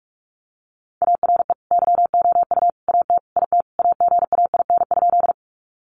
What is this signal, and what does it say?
Morse code, heard as a keyed tone, about 44 wpm, 700 Hz. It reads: ARE YOU AT IT AGAIN?
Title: Sentences from Top 200 Words 45wpm